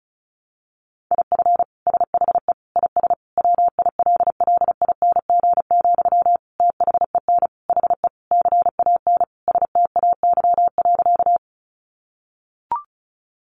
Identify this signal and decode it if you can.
Morse code, heard as a keyed tone, about 34 wpm, 700 Hz. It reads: IF SHE IS WILLING, THEN HE CAN STAY.
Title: Sentences from Top 1000 Words 35wpm